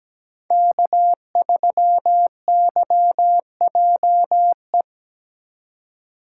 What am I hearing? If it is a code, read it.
Morse code, heard as a keyed tone, about 17 wpm, 700 Hz. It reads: K3YJE
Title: USA Call Signs - 5 Characters - 4x Character Spacing - 17wpm